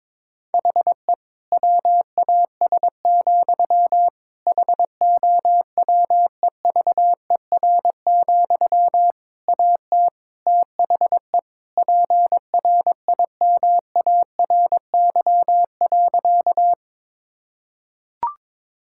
Morse code, heard as a keyed tone, about 22 wpm, 700 Hz. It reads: HE WAS, HOWEVER, AT THE PRIMARY.
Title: Sentences from Top 1000 Words - Review Part 5 - 22wpm